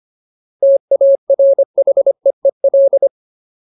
Morse code, heard as a keyed tone, about 25 wpm, 550 Hz. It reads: TARHEEL
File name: Ultimate Word List / 25WPM Ultimate